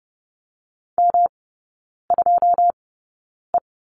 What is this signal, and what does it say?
Morse code, heard as a keyed tone, about 30 wpm, spaced as for 9 wpm, 700 Hz. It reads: M2E
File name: Lesson 18e - Letter m - Three Characters 30wpm